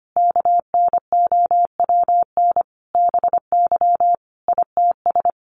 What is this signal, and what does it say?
Morse code, heard as a keyed tone, about 25 wpm, 700 Hz. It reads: KNOWN BY ITS
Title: Sentences from Top 1300 Words 25wpm